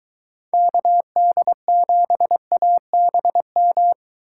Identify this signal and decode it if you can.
Morse code, heard as a keyed tone, about 23 wpm, 700 Hz. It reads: KD7ABM